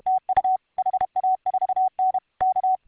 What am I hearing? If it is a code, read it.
Morse code, heard as a keyed tone, about 32 wpm, 750 Hz. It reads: TU HA4N K